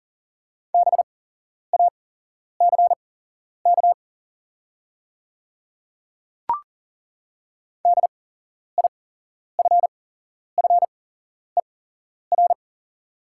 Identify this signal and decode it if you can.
Morse code, heard as a keyed tone, about 40 wpm, spaced as for 11 wpm, 700 Hz. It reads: BACK DIFFER
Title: 8x Character Spacing - Top 200 Words - 40wpm